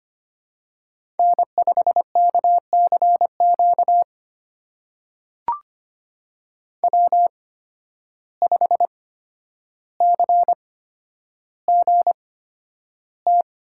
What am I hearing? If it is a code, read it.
Morse code, heard as a keyed tone, about 25 wpm, spaced as for 11 wpm, 700 Hz. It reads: N5KCQ W5CGT
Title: USA Call Signs - 5 Characters - 8x Character Spacing - 25wpm